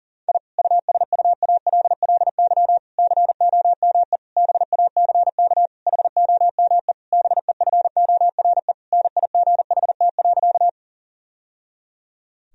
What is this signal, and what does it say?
Morse code, heard as a keyed tone, about 40 wpm, 700 Hz. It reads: I USUALLY COME BACK HOME BEFORE NIGHT.
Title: Sentences from Top 1000 Words - Review Part 3 - 40wpm